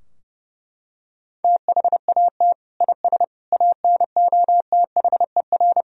Morse code, heard as a keyed tone, about 30 wpm, 700 Hz. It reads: THAT IS ANOTHER
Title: Sentences from Top 1000 Words - Review Part 2 - 30wpm